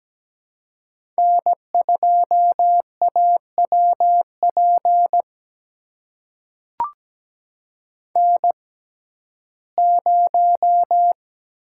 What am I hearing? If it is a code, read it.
Morse code, heard as a keyed tone, about 17 wpm, spaced as for 12 wpm, 700 Hz. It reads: N2AWP N0
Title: USA Call Signs - 5 Characters - 6x Character Spacing - 17wpm